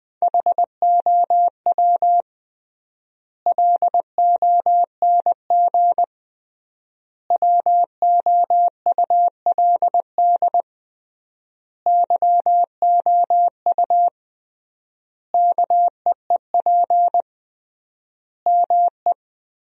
Morse code, heard as a keyed tone, about 21 wpm, 700 Hz. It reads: HOW LONG WOULD YOU KEEP ME
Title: 3x Word-Spacing - Sentences from Top 500 Words - 20wpm